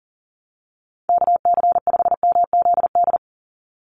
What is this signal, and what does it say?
Morse code, heard as a keyed tone, about 40 wpm, 700 Hz. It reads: KC5MZD
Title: USA Call Signs 40wpm